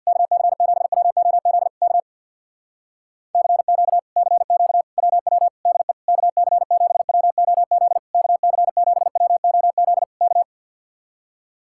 Morse code, heard as a keyed tone, about 55 wpm, 700 Hz. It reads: KC7WOZ K CQ CQ WW DE KC7WOZ KC7WOZ K
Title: Speed-Racing Common QSO Phrases - 50wpm